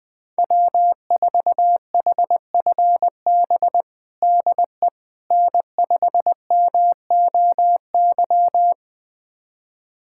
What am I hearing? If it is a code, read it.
Morse code, heard as a keyed tone, about 20 wpm, 700 Hz. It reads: W4HFB DE N5MOY
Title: USA Callsign DE Callsign - 5 Characters - 20wpm